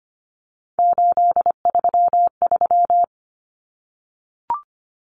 Morse code, heard as a keyed tone, about 25 wpm, 700 Hz. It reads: 833
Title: Common Cut Numbers - 3-Digit 25wpm